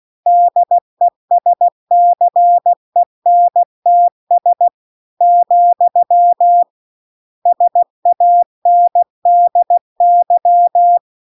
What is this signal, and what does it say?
Morse code, heard as a keyed tone, about 16 wpm, 700 Hz. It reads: DESCENTS , SANDY